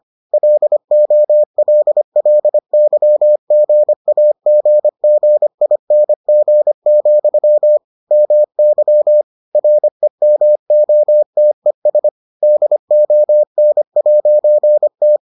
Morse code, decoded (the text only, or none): LOLLYGAGGING, MY REMOTES DON'T